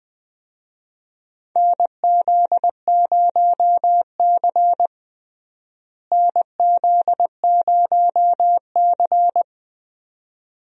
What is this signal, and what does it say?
Morse code, heard as a keyed tone, about 21 wpm, 700 Hz. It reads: NZ0C NZ0C
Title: USA Call Signs - 4 Characters - Repeated 2x - 20wpm